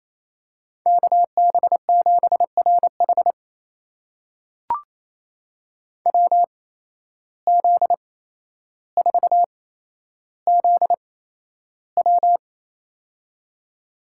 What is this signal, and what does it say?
Morse code, heard as a keyed tone, about 28 wpm, spaced as for 12 wpm, 700 Hz. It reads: KB7RH WZ4ZW
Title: USA Call Signs - 5 Characters - 8x Character Spacing - 28wpm